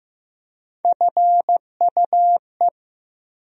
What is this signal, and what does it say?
Morse code, heard as a keyed tone, about 15 wpm, 700 Hz. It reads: FUE
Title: USA Call Sign Postfixes - 3 Letters 15wpm